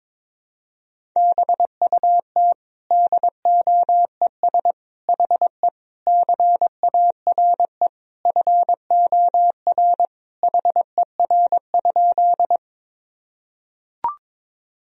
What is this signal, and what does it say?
Morse code, heard as a keyed tone, about 22 wpm, 700 Hz. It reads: BUT DOES HE CARE FOR HER?
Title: Sentences from Top 1000 Words - Review Part 1 - 22wpm